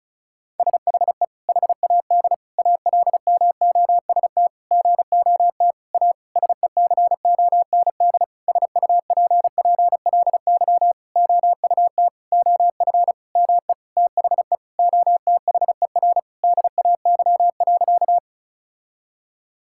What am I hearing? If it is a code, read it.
Morse code, heard as a keyed tone, about 36 wpm, 700 Hz. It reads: SHE HAD ALMOST GOT A SECOND SUPPLY OUT OF ME THE OTHER DAY.
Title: Sentences from Top 1000 Words - Review Part 5 - 35wpm